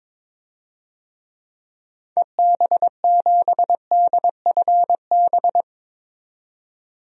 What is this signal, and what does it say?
Morse code, heard as a keyed tone, about 22 wpm, 700 Hz. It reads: EB7DFB
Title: DXCC-100 Call Signs - 6 Characters - 22wpm